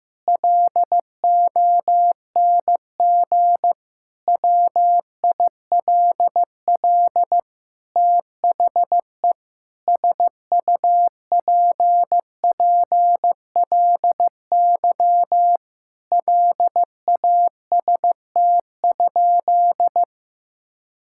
Morse code, heard as a keyed tone, about 15 wpm, 700 Hz. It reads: LONG WILL THE SUPPLY LAST?
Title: Sentences from Top 1000 Words - Review Part 5 - 15wpm